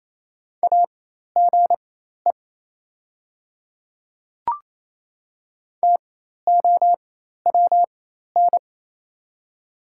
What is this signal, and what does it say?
Morse code, heard as a keyed tone, about 28 wpm, spaced as for 13 wpm, 700 Hz. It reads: AGE TOWN
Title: 4x Character Spacing - Top 500 Words - 28wpm